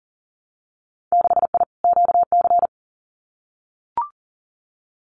Morse code, heard as a keyed tone, about 40 wpm, 700 Hz. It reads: 6I QC
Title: Winter Field Day Exchanges Encore 40wpm